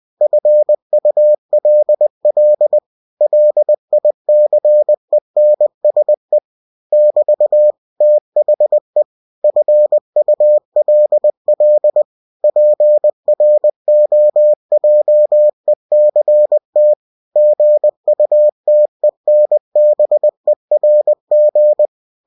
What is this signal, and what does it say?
Morse code, heard as a keyed tone, about 20 wpm, 600 Hz. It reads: FULL LICENSE <BT> THE FULL PROJECT GUTENBERG
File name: cw-books / winnie-the-pooh / winnie-20wpm-page-0112